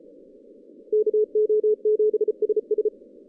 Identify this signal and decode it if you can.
Morse code, heard as a keyed tone, about 34 wpm, 430 Hz. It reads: KO7SS
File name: 2015 cq ww cw / 2015 cqwww k5tr - 0-18